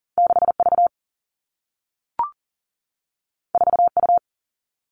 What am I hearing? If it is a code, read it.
Morse code, heard as a keyed tone, about 40 wpm, 700 Hz. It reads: 6V 4U
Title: Two Challenging Letter-Number Combinations 40wpm